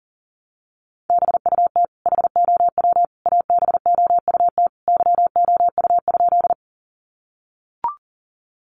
Morse code, heard as a keyed tone, about 40 wpm, 700 Hz. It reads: BUT HOW ABOUT YOU?
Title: Sentences from Top 100 Words 40wpm